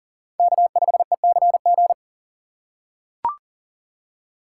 Morse code, heard as a keyed tone, about 40 wpm, 700 Hz. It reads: K5ECG
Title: USA Call Signs - 5 Characters - 3x Character Spacing - 40wpm